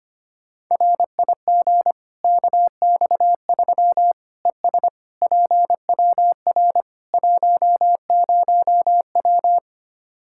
Morse code, heard as a keyed tone, about 25 wpm, 700 Hz. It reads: RIG KX3 ES PWR 10W
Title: Practice QSO - Exchange 2 - Info - 25wpm